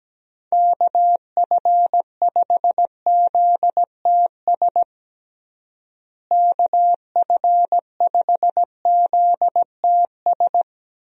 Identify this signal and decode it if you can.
Morse code, heard as a keyed tone, about 17 wpm, 700 Hz. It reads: KF5ZTS KF5ZTS
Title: USA Call Signs - 6 Characters - Repeated 3x - 17wpm